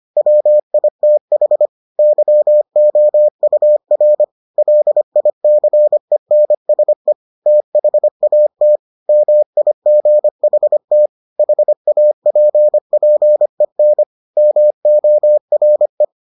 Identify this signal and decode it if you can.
Morse code, heard as a keyed tone, about 25 wpm, 600 Hz. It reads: WITH YOUR LICENSE THAT MIGHT HAPPEN MORE